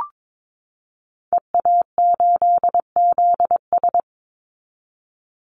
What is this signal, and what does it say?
Morse code, heard as a keyed tone, about 22 wpm, 700 Hz. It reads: EA8ZS
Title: Call Signs Repeated 3x - 22wpm